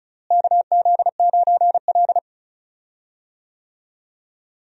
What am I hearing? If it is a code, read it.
Morse code, heard as a keyed tone, about 34 wpm, 700 Hz. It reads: KZ9L
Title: USA Call Signs - 4 Characters - Repeated 2x - 35wpm